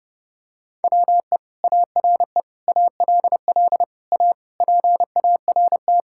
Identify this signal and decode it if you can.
Morse code, heard as a keyed tone, about 30 wpm, 700 Hz. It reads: WE ARE ALL A PART
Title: Sentences from Top 1000 Words - Review Part 6 - 30wpm